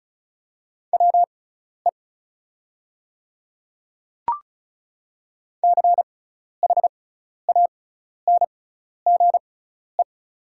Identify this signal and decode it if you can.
Morse code, heard as a keyed tone, about 34 wpm, spaced as for 12 wpm, 700 Hz. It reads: WE CHANGE
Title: 6x Character Spacing - Top 200 Words - 35wpm